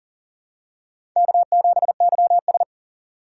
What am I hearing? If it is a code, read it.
Morse code, heard as a keyed tone, about 38 wpm, 700 Hz. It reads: K7YS
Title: USA Call Signs - 4 Characters - 4x Character Spacing - 40wpm